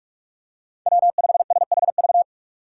Morse code, heard as a keyed tone, about 42 wpm, 700 Hz. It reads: W5SHV
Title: USA Call Signs - 6x Character Spacing - 45wpm